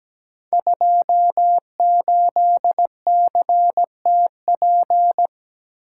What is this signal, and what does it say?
Morse code, heard as a keyed tone, about 17 wpm, 700 Hz. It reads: 28CTP
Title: Code Groups - 5 Letter:Numbers - 17wpm